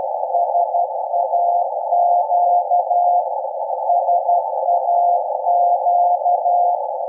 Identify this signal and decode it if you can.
Morse code, heard as a keyed tone, about 12 wpm, 700 Hz, with noise. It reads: SAQ SAQ
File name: rec / 24-Dec-2012-0856 0.020MHz